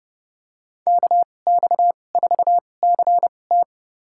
Morse code, heard as a keyed tone, about 31 wpm, spaced as for 22 wpm, 700 Hz. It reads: KX4CT
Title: USA Call Signs - 5 Characters - 2x Character Spacing - 30wpm